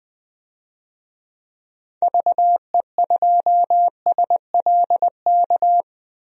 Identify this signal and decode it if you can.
Morse code, heard as a keyed tone, about 21 wpm, 700 Hz. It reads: VE2SLK
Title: DXCC-100 Call Signs - 6 Characters - 20wpm